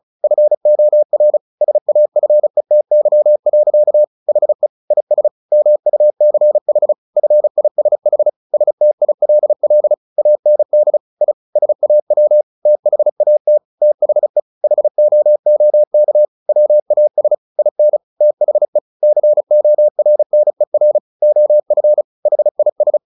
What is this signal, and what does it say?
Morse code, heard as a keyed tone, about 34 wpm, 600 Hz. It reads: FOR SAFETY. HE IS MUCH FISH STILL AND I SAW THAT THE HOOK WAS IN THE CORNER OF HIS
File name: the old man and the sea 2 35wpm